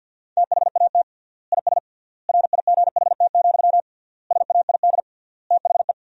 Morse code, heard as a keyed tone, about 50 wpm, 700 Hz. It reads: THAT IS RIGHT, SAID THE
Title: 3x Word-Spacing - Sentences from Top 200 Words - 50wpm